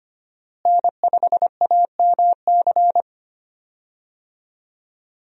Morse code, heard as a keyed tone, about 25 wpm, 690 Hz. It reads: N5AMC
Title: USA Call Signs - 5 Characters - Repeated 2x - 25wpm